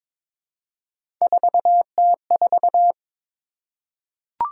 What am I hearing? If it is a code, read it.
Morse code, heard as a keyed tone, about 22 wpm, 700 Hz. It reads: 4T4
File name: practice / 22wpm / Three Letter-Number Combinations - 22wpm